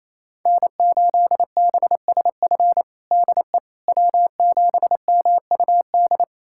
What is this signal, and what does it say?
Morse code, heard as a keyed tone, about 27 wpm, 700 Hz. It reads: N8BSF DE W7MUD
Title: USA Callsign DE Callsign - 5 Characters - 28wpm